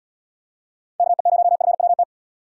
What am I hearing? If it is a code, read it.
Morse code, heard as a keyed tone, about 78 wpm, 700 Hz. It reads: KE0FWT